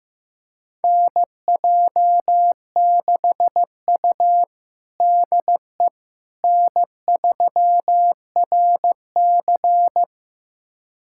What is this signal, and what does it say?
Morse code, heard as a keyed tone, about 16 wpm, 700 Hz. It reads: NJ6U DE N3RC
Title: USA Callsign DE Callsign - 4 Characters - 15wpm